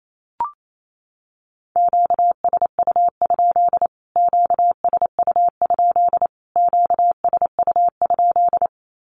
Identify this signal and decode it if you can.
Morse code, heard as a keyed tone, about 28 wpm, 700 Hz. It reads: QSU? QSU? QSU?
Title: Q-Codes Repeated 3x 28wpm